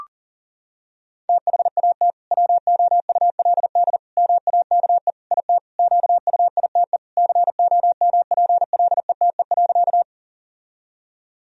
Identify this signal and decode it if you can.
Morse code, heard as a keyed tone, about 40 wpm, 700 Hz. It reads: THAT WOULD MAKE IT QUITE COMPLETE.
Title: Sentences from Top 1000 Words - Review Part 2 - 40wpm